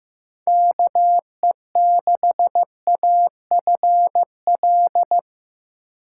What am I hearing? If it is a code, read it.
Morse code, heard as a keyed tone, about 16 wpm, 700 Hz. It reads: KE6AFL
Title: USA Call Signs - 6 Characters - 4x Character Spacing - 15wpm